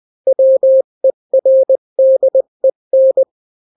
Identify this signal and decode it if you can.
Morse code, heard as a keyed tone, about 20 wpm, 530 Hz. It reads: WERDEN
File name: daily / 18wpm